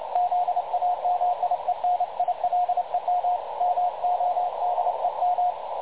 Morse code, heard as a keyed tone, about 29 wpm, 720 Hz, with radio noise. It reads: ZAMIENILEM MODEM
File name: dlazaawansowanych / nagraniazpasma / websdr recording 3565.0kHz